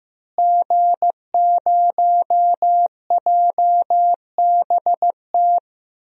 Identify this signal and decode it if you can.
Morse code, heard as a keyed tone, about 14 wpm, 700 Hz. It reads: G0JBT
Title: DXCC-100 Call Signs - 5 Characters - Repeated 3x - 15wpm